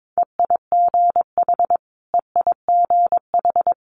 Morse code, heard as a keyed tone, about 22 wpm, 700 Hz. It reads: EIGH EIGH
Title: Multi-letter Phonemes Repeated 3x 22wpm